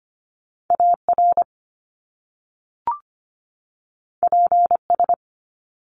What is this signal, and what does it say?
Morse code, heard as a keyed tone, about 25 wpm, 700 Hz. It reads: AR PS
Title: Top Two Letter Combinations in Words 25wpm